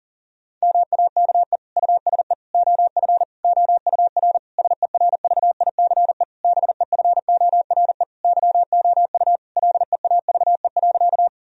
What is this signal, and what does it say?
Morse code, heard as a keyed tone, about 40 wpm, 700 Hz. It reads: MAKE USE OF OUR SERVICE BEFORE YOU LEAVE.
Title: Sentences from Top 1000 Words - Review Part 4 - 40wpm